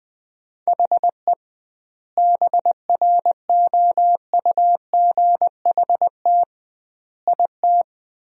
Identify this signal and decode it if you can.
Morse code, heard as a keyed tone, about 20 wpm, 700 Hz. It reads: HE BROUGHT IT
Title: 2x Word-Spacing - Sentences from Top 500 Words - 20wpm